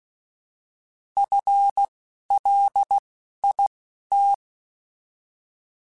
Morse code, heard as a keyed tone, about 16 wpm, spaced as for 11 wpm, 800 Hz. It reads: FLIT
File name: FL / FL mp3s / FL Prefixes 3x2 sp @10